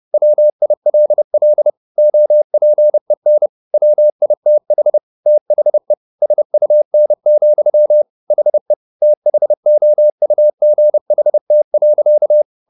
Morse code, heard as a keyed tone, about 30 wpm, 600 Hz. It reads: WILL OPEN WITH THE SUN, HE THOUGHT.
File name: the old man and the sea / the old man and the sea 1 30wpm